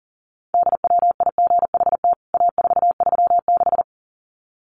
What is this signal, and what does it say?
Morse code, heard as a keyed tone, about 40 wpm, 700 Hz. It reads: DWIGHT A436
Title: CWT Exchange 40wpm